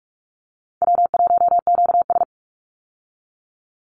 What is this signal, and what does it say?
Morse code, heard as a keyed tone, about 44 wpm, 700 Hz. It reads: W1QS
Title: Easy Call Signs Repeated 2x - 45wpm